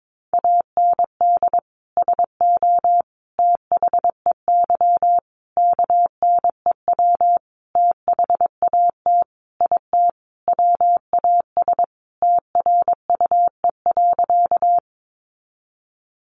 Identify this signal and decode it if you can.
Morse code, heard as a keyed tone, about 22 wpm, 700 Hz. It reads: AND SO THEY KNEW THAT IT WAS TRUE.
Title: Sentences from Top 400 Words 22wpm